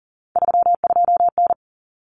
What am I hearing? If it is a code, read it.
Morse code, heard as a keyed tone, about 40 wpm, 700 Hz. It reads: 32N
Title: Common Cut Numbers - 3-Digit 40wpm